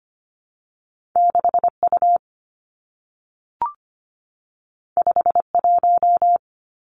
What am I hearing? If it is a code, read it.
Morse code, heard as a keyed tone, about 25 wpm, 700 Hz. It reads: 6 U 5 1
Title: Two Challenging Letter-Number Combinations 25wpm